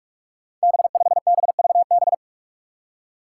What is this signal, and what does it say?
Morse code, heard as a keyed tone, about 44 wpm, 700 Hz. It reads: B5BVB
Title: Five Challenging Letter-Number Combinations 45wpm